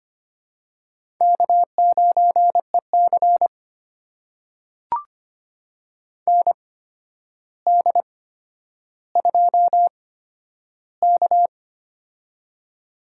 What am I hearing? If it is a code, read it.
Morse code, heard as a keyed tone, about 25 wpm, spaced as for 11 wpm, 700 Hz. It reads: K9EC ND2K
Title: USA Call Signs - 8x Character Spacing - 25wpm